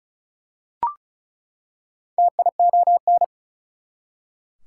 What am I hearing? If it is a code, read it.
Morse code, heard as a keyed tone, about 36 wpm, 700 Hz. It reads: TION
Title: Prefixes and Suffixes 35wpm